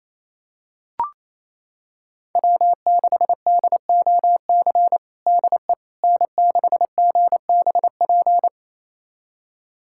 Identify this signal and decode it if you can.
Morse code, heard as a keyed tone, about 28 wpm, 700 Hz. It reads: W6DOC DE N6GBP
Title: USA Callsign DE Callsign - 5 Characters - 28wpm